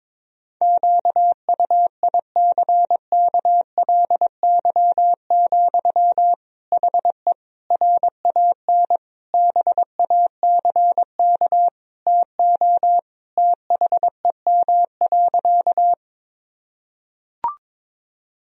Morse code, heard as a keyed tone, about 22 wpm, 700 Hz. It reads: QUICKLY, HE RAN BACK TO THEM.
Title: Sentences from Top 1000 Words - Review Part 3 - 22wpm